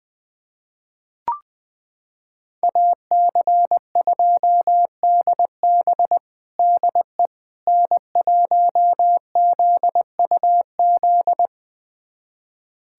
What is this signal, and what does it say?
Morse code, heard as a keyed tone, about 21 wpm, 700 Hz. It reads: AC2DB DE N1ZUZ
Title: USA Callsign DE Callsign - 5 Characters - 20wpm